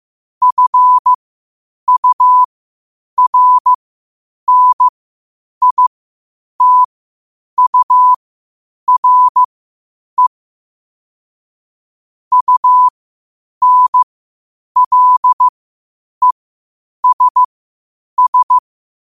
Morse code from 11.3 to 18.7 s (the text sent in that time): UNLESS